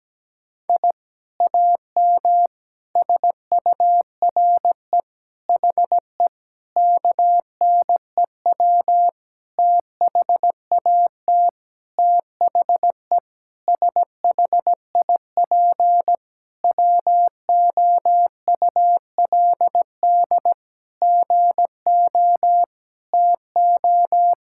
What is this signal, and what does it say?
Morse code, heard as a keyed tone, about 17 wpm, 700 Hz. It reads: I AM SURE HE KNEW THAT THE SHIP WOULD GO TO
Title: Sentences from Top 1000 Words - Review Part 6 - 17wpm